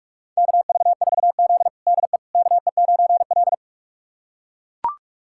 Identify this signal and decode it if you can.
Morse code, heard as a keyed tone, about 42 wpm, 700 Hz. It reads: KV4Z DE KE9L
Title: USA Callsign DE Callsign - 4 Characters - 45wpm